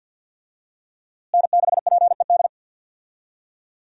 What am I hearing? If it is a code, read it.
Morse code, heard as a keyed tone, about 46 wpm, 700 Hz. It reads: N6PED
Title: USA Call Signs - 5 Characters - Repeated 2x - 50wpm